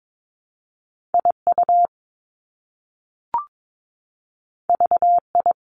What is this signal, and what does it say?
Morse code, heard as a keyed tone, about 22 wpm, 700 Hz. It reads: IU VI
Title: Top Two Letter Combinations in Words 22wpm